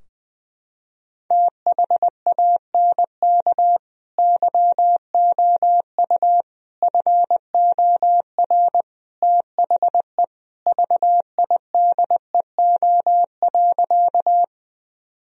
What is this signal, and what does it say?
Morse code, heard as a keyed tone, about 20 wpm, 700 Hz. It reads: THANK YOU FOR THE VIDEO.
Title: Sentences from Top 1000 Words - Review Part 4 - 20wpm